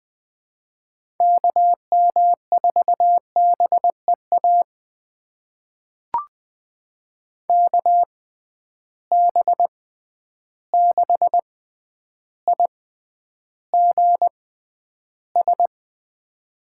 Morse code, heard as a keyed tone, about 20 wpm, spaced as for 10 wpm, 700 Hz. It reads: KM4BEA KB6IGS
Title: USA Call Signs - 6 Characters - 6x Character Spacing - 20wpm